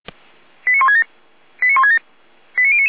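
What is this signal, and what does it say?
Five-tone selective call (ZVEI), heard as quick runs of five tones, tones of about 80 ms. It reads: 89167 89167 89099
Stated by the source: ZVEI five-tone selective call 89163 received on an SDR.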